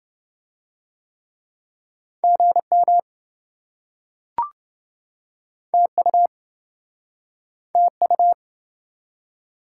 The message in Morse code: GM TU TU

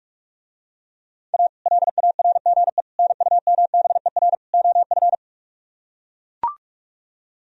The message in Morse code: A LARGE NUMBER OF

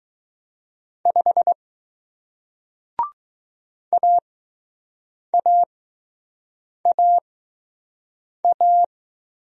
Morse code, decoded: 5 AAAA

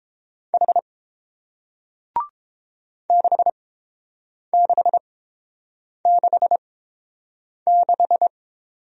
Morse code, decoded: H 6666

